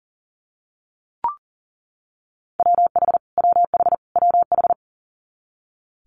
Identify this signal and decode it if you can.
Morse code, heard as a keyed tone, about 38 wpm, 700 Hz. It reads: WH WH WH
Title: Multi-letter Phonemes Repeated 3x 40wpm